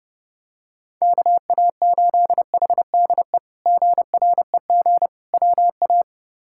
Morse code, heard as a keyed tone, about 32 wpm, 700 Hz. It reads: KA8HDE GREG WA
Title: K1USN SST QSO Practice - 30wpm